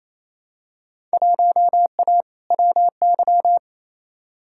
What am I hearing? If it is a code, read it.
Morse code, heard as a keyed tone, about 28 wpm, 700 Hz. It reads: 1A WY
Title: Field Day Exchanges Encore - 28wpm